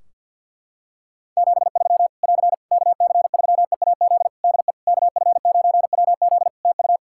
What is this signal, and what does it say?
Morse code, heard as a keyed tone, about 52 wpm, 700 Hz. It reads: 73 <AR> KC3EAZ DE KF9WZ TU
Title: Practice QSO - Exchange 4 - Ending - 50wpm